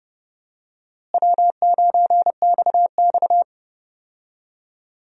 Morse code, heard as a keyed tone, about 30 wpm, 700 Hz. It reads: W9XX
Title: USA Call Signs - 4 Characters - Repeated 3x - 30wpm